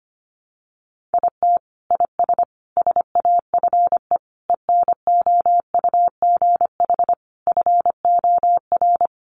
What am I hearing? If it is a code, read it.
Morse code, heard as a keyed tone, about 25 wpm, 700 Hz. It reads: IT IS SAFE ENOUGH FOR
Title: Sentences from Top 900 Words 25wpm